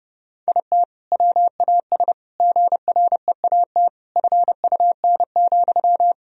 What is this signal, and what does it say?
Morse code, heard as a keyed tone, about 30 wpm, 700 Hz. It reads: IT WAS GREAT FUN,
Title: Sentences from Top 1000 Words - Review Part 4 - 30wpm